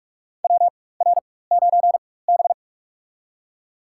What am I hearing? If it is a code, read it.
Morse code, heard as a keyed tone, about 44 wpm, spaced as for 21 wpm, 700 Hz. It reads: WR9B
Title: USA Call Signs - 4 Characters - 4x Character Spacing - 45wpm